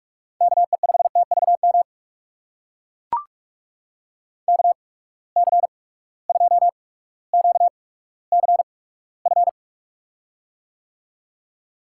Morse code, heard as a keyed tone, about 44 wpm, spaced as for 19 wpm, 700 Hz. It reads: KE5TVM KC2QCF
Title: USA Call Signs - 6 Characters - 8x Character Spacing - 45wpm